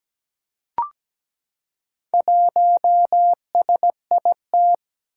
Morse code, heard as a keyed tone, about 17 wpm, 700 Hz. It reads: 1SIT
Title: Four Challenging Letter-Number Combinations 17wpm